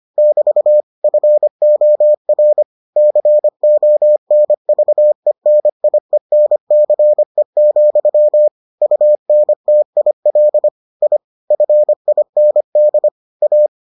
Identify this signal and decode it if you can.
Morse code, heard as a keyed tone, about 25 wpm, 600 Hz. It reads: <BT> FOR CONVENIENCE, UNTIL I FIND A